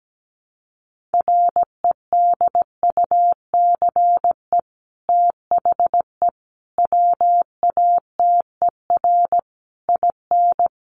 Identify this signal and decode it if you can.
Morse code, heard as a keyed tone, about 17 wpm, 700 Hz. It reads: REDUCE THE WATER IN
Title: Sentences from Top 1100 Words 17wpm